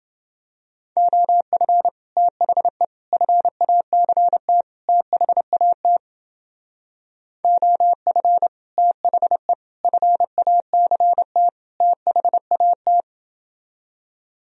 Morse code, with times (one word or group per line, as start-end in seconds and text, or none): 1.0-1.9 OF
2.2-2.8 THE
3.1-4.6 FACT
4.9-6.0 THAT
7.4-8.5 OF
8.8-9.5 THE
9.8-11.5 FACT
11.8-13.0 THAT